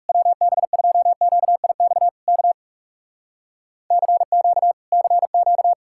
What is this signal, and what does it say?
Morse code, heard as a keyed tone, about 40 wpm, 700 Hz. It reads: WB2QIX K CQ CQ